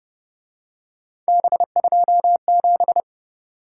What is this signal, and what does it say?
Morse code, heard as a keyed tone, about 28 wpm, 700 Hz. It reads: B27